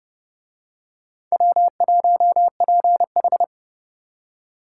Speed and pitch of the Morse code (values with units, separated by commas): 29 wpm, 700 Hz